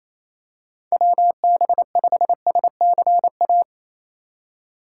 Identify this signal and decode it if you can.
Morse code, heard as a keyed tone, about 27 wpm, 700 Hz. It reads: WB5SCA
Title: USA Call Signs - 4x Character Spacing - 28wpm